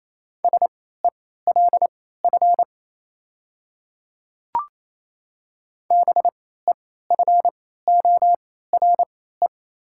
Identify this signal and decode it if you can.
Morse code, heard as a keyed tone, about 27 wpm, spaced as for 16 wpm, 700 Hz. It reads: SELF BEFORE
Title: 3x Character Spacing - Top 300 Words - 28wpm